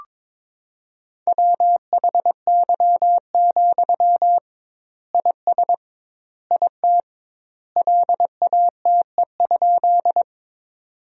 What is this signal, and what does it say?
Morse code, heard as a keyed tone, about 22 wpm, 700 Hz. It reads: WHY, IS IT LATE?
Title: 2x Word-Spacing - Sentences from Top 300 Words - 22wpm